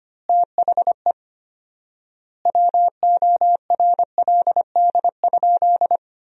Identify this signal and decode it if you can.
Morse code, heard as a keyed tone, about 25 wpm, 700 Hz. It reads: THE WORLD?
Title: 4x Word-Spacing - Sentences from Top 200 Words - 25wpm